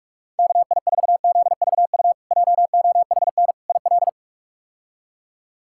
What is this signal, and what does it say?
Morse code, heard as a keyed tone, about 47 wpm, 700 Hz. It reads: KI4ZVU JOHN IL
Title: K1USN SST QSO Practice - 45wpm